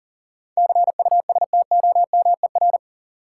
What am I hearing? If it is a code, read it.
Morse code, heard as a keyed tone, about 39 wpm, 700 Hz. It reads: CUSTOMER
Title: Sentences from Top 1100 Words - New Word Drill - 40wpm